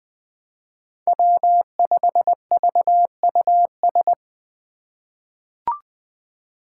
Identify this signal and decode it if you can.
Morse code, heard as a keyed tone, about 20 wpm, 700 Hz. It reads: W5VUS
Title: Five Challenging Letter-Number Combinations 20wpm